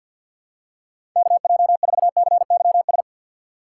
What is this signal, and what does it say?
Morse code, heard as a keyed tone, about 46 wpm, 700 Hz. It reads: KJ4CYS
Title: USA Call Signs - 6 Characters - 3x Character Spacing - 50wpm